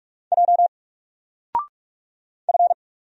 Morse code, heard as a keyed tone, about 46 wpm, 700 Hz. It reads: J F
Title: Single Character (Letter:Number:Punctuation:Prosign) 45wpm